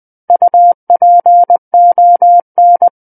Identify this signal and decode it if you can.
Morse code, heard as a keyed tone, about 20 wpm, 700 Hz. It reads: UPON
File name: 20 WPM